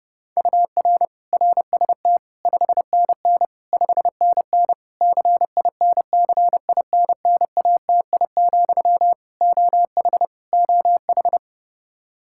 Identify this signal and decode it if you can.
Morse code, heard as a keyed tone, about 30 wpm, 700 Hz. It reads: UR RST 5NN 5NN CINCINNATI, OH OH
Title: Practice QSO - Exchange 1 - Beginning - 30wpm